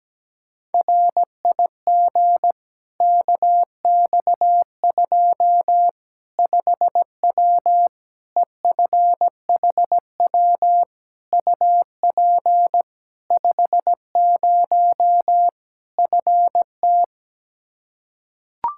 Morse code, heard as a keyed tone, about 17 wpm, 700 Hz. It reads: RIG KX2 5W EFHW UP 50 FT
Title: Practice QSO - Exchange 2 - Info - 17wpm